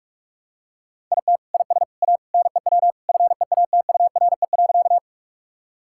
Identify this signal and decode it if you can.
Morse code, heard as a keyed tone, about 42 wpm, 700 Hz. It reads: IT IS A NEW FEATURE.
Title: Sentences from Top 1000 Words - Review Part 5 - 45wpm